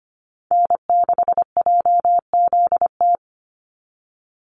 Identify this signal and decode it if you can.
Morse code, heard as a keyed tone, about 25 wpm, 700 Hz. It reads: N6JZT
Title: NA SOTA Chaser Callsigns 25wpm